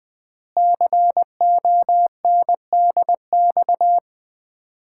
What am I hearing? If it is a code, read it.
Morse code, heard as a keyed tone, about 20 wpm, 700 Hz. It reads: CONDX
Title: Common QSO Phrases - 4x Word-Spacing - 20wpm